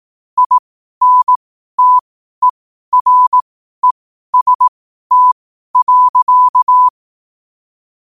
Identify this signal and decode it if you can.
Morse code, heard as a keyed tone, about 18 wpm, spaced as for 12 wpm, 1000 Hz. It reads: INTEREST.
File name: 12wpm